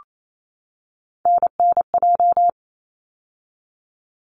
Morse code, heard as a keyed tone, about 28 wpm, 700 Hz. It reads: NNJ
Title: ARRL Sections 28wpm